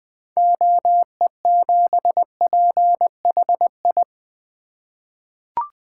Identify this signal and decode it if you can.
Morse code, heard as a keyed tone, about 20 wpm, 700 Hz. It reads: OE7PHI
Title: 2023 SOTA Chaser Call Signs - 20wpm